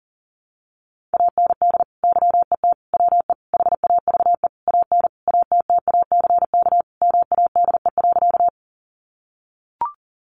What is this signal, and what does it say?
Morse code, heard as a keyed tone, about 40 wpm, 700 Hz. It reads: AND YET WE HAVE AN ATTACK MADE.
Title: Sentences from Top 1500 Words 40wpm